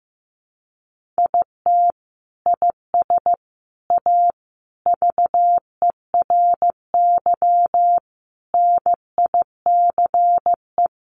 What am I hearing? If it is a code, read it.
Morse code, heard as a keyed tone, about 15 wpm, 700 Hz. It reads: IT IS A VERY NICE
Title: Sentences from Top 1100 Words 15wpm